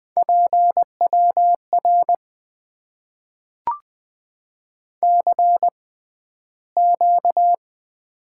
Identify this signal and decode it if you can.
Morse code, heard as a keyed tone, about 20 wpm, spaced as for 12 wpm, 700 Hz. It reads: PWR CQ
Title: Top 90 QSO Elements - 6x Character Spacing - 20wpm